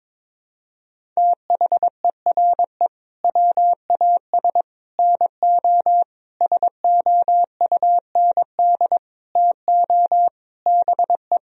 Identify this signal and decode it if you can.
Morse code, heard as a keyed tone, about 22 wpm, 700 Hz. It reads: THERE WAS NO SOUND TO BE
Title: Sentences from Top 1000 Words - Review Part 1 - 22wpm